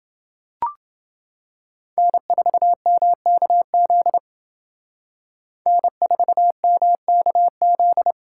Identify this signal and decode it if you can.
Morse code, heard as a keyed tone, about 29 wpm, 700 Hz. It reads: N4MKZ N4MKZ